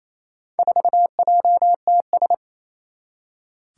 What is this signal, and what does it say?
Morse code, heard as a keyed tone, about 28 wpm, 700 Hz. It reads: 4JTS
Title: Four Challenging Letter-Number Combinations 28wpm